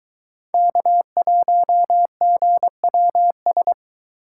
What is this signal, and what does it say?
Morse code, heard as a keyed tone, about 19 wpm, 700 Hz. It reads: K1GWS